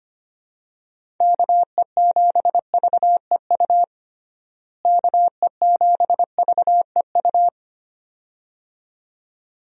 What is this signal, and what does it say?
Morse code, heard as a keyed tone, about 25 wpm, 700 Hz. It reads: KE7VEU KE7VEU
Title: USA Call Signs - 6 Characters - Repeated 2x - 25wpm